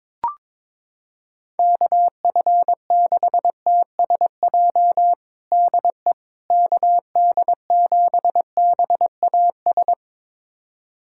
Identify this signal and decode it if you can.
Morse code, heard as a keyed tone, about 22 wpm, 700 Hz. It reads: KF6TSJ DE KD7BAS
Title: USA Callsign DE Callsign - 6 Characters - 22wpm